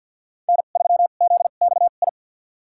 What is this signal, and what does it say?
Morse code, heard as a keyed tone, about 48 wpm, spaced as for 35 wpm, 700 Hz. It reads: N3ZXI
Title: USA Call Signs - 5 Characters - 2x Character Spacing - 50wpm